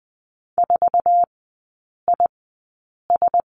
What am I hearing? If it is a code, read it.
Morse code, heard as a keyed tone, about 20 wpm, spaced as for 8 wpm, 700 Hz. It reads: 4IS
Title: 053 Lesson 10e - Letter r - Three Characters 20wpm